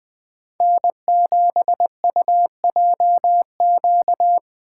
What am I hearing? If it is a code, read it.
Morse code, heard as a keyed tone, about 20 wpm, 700 Hz. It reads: N7UJQ
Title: USA Call Signs - 5 Characters - 3x Character Spacing - 20wpm